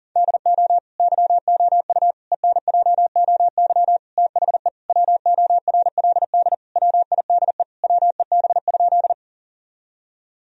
Morse code, heard as a keyed tone, about 37 wpm, 700 Hz. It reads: DO YOU ENJOY THE WORLD WIDE WEB?
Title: Sentences from Top 1000 Words - Review Part 4 - 40wpm